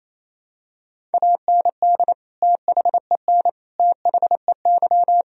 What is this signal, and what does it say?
Morse code, heard as a keyed tone, about 28 wpm, 700 Hz. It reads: AND THEN THEY
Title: Sentences from Top 1000 Words - Review Part 1 - 28wpm